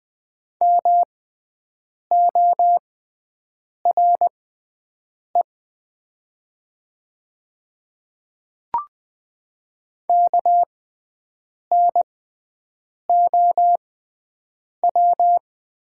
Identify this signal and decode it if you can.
Morse code, heard as a keyed tone, about 20 wpm, spaced as for 7 wpm, 690 Hz. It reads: MORE KNOW
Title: 6x Character Spacing - Top 200 Words - 20wpm